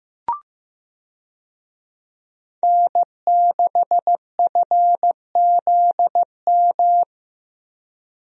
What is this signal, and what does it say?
Morse code, heard as a keyed tone, about 15 wpm, 700 Hz. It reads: N6FZM
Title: USA Call Signs - 5 Characters - Repeated 3x - 15wpm